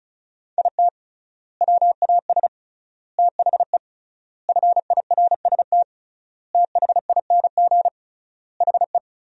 Morse code, heard as a keyed tone, about 36 wpm, 700 Hz. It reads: IT WAS THE FIRST THING HE
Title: 3x Word-Spacing - Sentences from Top 300 Words - 35wpm